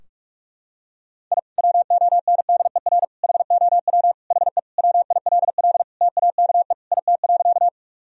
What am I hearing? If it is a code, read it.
Morse code, heard as a keyed tone, about 45 wpm, 700 Hz. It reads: I WONDER HOW HE WILL TAKE IT.
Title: Sentences from Top 1000 Words - Review Part 3 - 45wpm